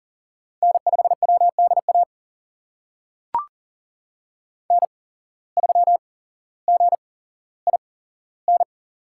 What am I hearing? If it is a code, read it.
Morse code, heard as a keyed tone, about 40 wpm, spaced as for 17 wpm, 700 Hz. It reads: N5WDA N3GIN
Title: USA Call Signs - 5 Characters - 8x Character Spacing - 40wpm